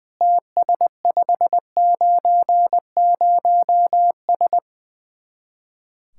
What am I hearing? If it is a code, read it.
Morse code, heard as a keyed tone, about 20 wpm, 700 Hz. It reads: TS590S
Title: Speed-Racing Common QSO Phrases - 20wpm